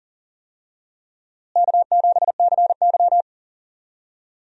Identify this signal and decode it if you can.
Morse code, heard as a keyed tone, about 40 wpm, 700 Hz. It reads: K7CY
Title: USA Call Signs - 4 Characters - Repeated 3x - 40wpm